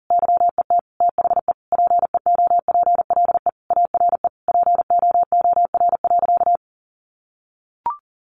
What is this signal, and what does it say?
Morse code, heard as a keyed tone, about 40 wpm, 700 Hz. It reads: YET THE PEOPLE ARE POOR.
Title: Sentences from Top 1100 Words 40wpm